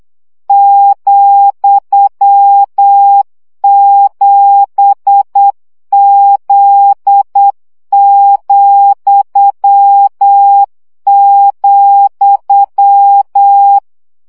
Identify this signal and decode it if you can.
Morse code, heard as a keyed tone, about 8 wpm, 800 Hz. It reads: ,7Z,,
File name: CW gyakorlas z 7 vesszo 7 baud